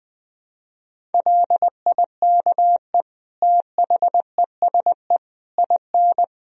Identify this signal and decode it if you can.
Morse code, heard as a keyed tone, about 20 wpm, 700 Hz. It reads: LIKE THESE IN
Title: Sentences from Top 500 Words and Abbreviations 20wpm